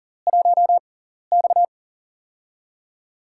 Morse code, heard as a keyed tone, about 39 wpm, spaced as for 14 wpm, 690 Hz. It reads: 1X
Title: USA Call Signs - 4 Characters - 6x Character Spacing - 40wpm